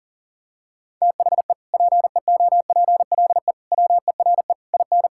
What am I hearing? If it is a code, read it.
Morse code, heard as a keyed tone, about 40 wpm, 700 Hz. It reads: THE PEOPLE WERE IN
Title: Sentences from Top 1600 Words 40wpm